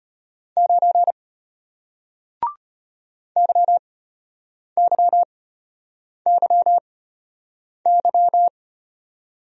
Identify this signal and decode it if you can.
Morse code, heard as a keyed tone, about 33 wpm, spaced as for 8 wpm, 700 Hz. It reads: 9 YYYY